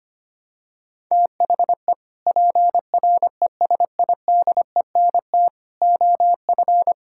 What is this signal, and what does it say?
Morse code, heard as a keyed tone, about 25 wpm, 700 Hz. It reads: THE PRESIDENT OF
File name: practice / 25wpm / Sentences with 3-Word Phrases - 25wpm